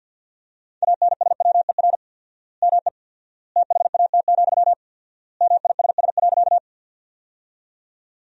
Morse code, heard as a keyed tone, about 53 wpm, 700 Hz. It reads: ANSWER ME THAT, MISS.
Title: 4x Word-Spacing - Sentences from Top 500 Words - 50wpm